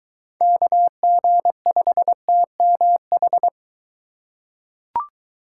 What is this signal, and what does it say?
Morse code, heard as a keyed tone, about 22 wpm, 700 Hz. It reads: KG5TMH